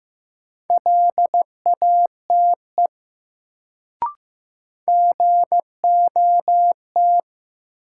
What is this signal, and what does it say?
Morse code, heard as a keyed tone, about 15 wpm, 700 Hz. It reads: LATE GOT
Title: Sentences from Top 300 Words - New Word Drill - 15wpm